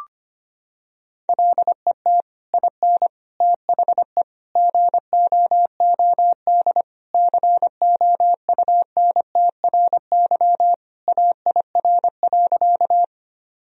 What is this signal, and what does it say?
Morse code, heard as a keyed tone, about 25 wpm, 700 Hz. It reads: LET IN THE GOOD COUNTRY AIR.
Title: Sentences from Top 1000 Words - Review Part 1 - 25wpm